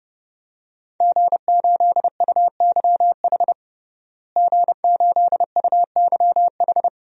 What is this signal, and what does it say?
Morse code, heard as a keyed tone, about 29 wpm, 700 Hz. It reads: G8UYH G8UYH
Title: DXCC-100 Call Signs - 5 Characters - Repeated 3x - 30wpm